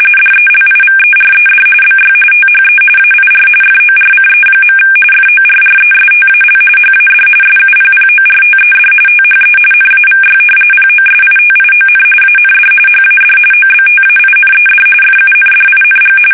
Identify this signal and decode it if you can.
RTTY, about 45.45 baud. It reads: DELIVERED BY THE AIR FORCE. FORRESTAL HAD SUPPORTED THE NAVY POSITION AND HAD OBTAINED FUNDING
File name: fsk850 / usnavy5